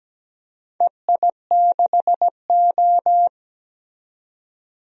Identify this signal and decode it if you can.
Morse code, heard as a keyed tone, about 17 wpm, 700 Hz. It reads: EI6O
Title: DXCC-100 Call Signs - 4 Characters - 17wpm